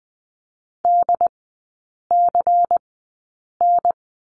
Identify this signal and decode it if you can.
Morse code, heard as a keyed tone, about 20 wpm, spaced as for 8 wpm, 700 Hz. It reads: DCN
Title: 093 Lesson 16e - Letter c - Three Characters 20wpm